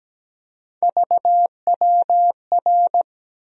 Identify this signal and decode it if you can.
Morse code, heard as a keyed tone, about 17 wpm, 700 Hz. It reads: VWR
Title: USA Call Sign Postfixes - 3 Letters 17wpm